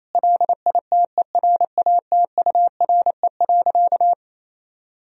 Morse code, heard as a keyed tone, about 27 wpm, 700 Hz. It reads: LITERATURE.
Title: Sentences from Top 1000 Words - Review Part 3 - 28wpm